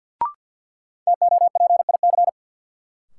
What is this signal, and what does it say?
Morse code, heard as a keyed tone, about 51 wpm, 700 Hz. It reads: TOPIC
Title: Sentences from Top 1200 Words - New Word Drill Alt - 50wpm